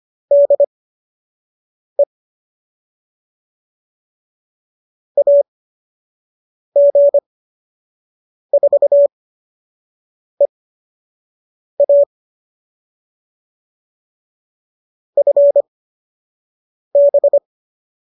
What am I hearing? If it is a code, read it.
Morse code, heard as a keyed tone, about 25 wpm, spaced as for 6 wpm, 570 Hz. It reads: DE AG4EA FB